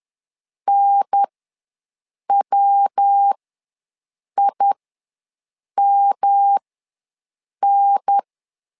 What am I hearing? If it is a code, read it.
Morse code, heard as a keyed tone, about 10 wpm, spaced as for 6 wpm, 800 Hz. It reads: NWIMN
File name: LEZIONI / 3 MWRO / 3 MW ws 10wpm 2213